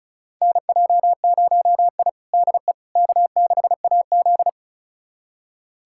Morse code, heard as a keyed tone, about 34 wpm, 700 Hz. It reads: NJ0I DE K6AZ
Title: USA Callsign DE Callsign - 4 Characters - 35wpm